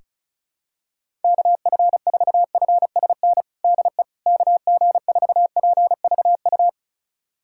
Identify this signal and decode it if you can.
Morse code, heard as a keyed tone, about 36 wpm, 700 Hz. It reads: KF4FSN DE KG4PVU
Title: USA Callsign DE Callsign - 6 Characters - 35wpm